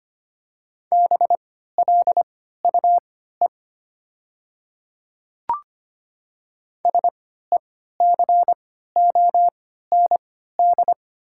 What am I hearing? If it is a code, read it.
Morse code, heard as a keyed tone, about 25 wpm, spaced as for 14 wpm, 700 Hz. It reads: BLUE SECOND
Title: 3x Character Spacing - Top 500 Words - 25wpm